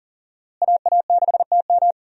Morse code, heard as a keyed tone, about 37 wpm, 700 Hz. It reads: AA6TM
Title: USA Call Signs - 5 Characters - Repeated 3x - 40wpm